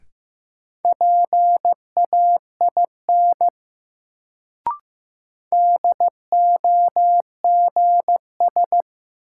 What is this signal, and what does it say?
Morse code, heard as a keyed tone, about 15 wpm, 700 Hz. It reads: PAIN DOGS
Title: Sentences from Top 1600 Words - New Word Drill - 15wpm